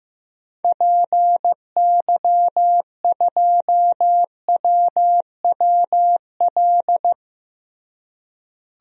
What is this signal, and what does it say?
Morse code, heard as a keyed tone, about 15 wpm, 700 Hz. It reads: PY2WWL
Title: DXCC-100 Call Signs - 6 Characters - Repeated 2x - 15wpm